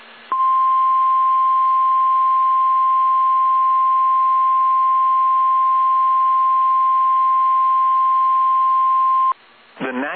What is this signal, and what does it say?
Emergency Alert System (SAME) tones (the 1050 Hz attention tone), with speech. It is heard from 0.3 to 9.4 s.